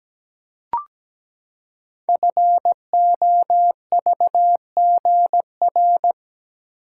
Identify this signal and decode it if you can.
Morse code, heard as a keyed tone, about 17 wpm, 700 Hz. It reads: FOVGR
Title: Code Groups - 5 Letters - 17wpm